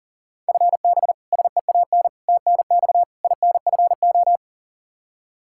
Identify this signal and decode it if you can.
Morse code, heard as a keyed tone, about 40 wpm, 700 Hz. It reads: FB SEAN TNX INFO
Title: Practice QSO - Exchange 3 - More Info - 40wpm